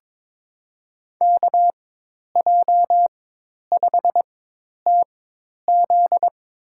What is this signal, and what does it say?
Morse code, heard as a keyed tone, about 22 wpm, spaced as for 10 wpm, 700 Hz. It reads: KJ5TZ
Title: USA Call Signs - 5 Characters - 4x Character Spacing - 22wpm